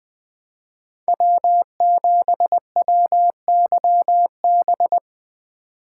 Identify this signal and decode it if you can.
Morse code, heard as a keyed tone, about 21 wpm, 700 Hz. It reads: W7WYB
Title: USA Call Signs - 5 Characters - 4x Character Spacing - 20wpm